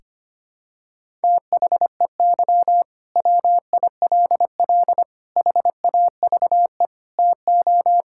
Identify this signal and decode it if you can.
Morse code, heard as a keyed tone, about 25 wpm, 700 Hz. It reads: THEY WILL HAVE TO
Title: Sentences from Top 1000 Words - Review Part 6 - 25wpm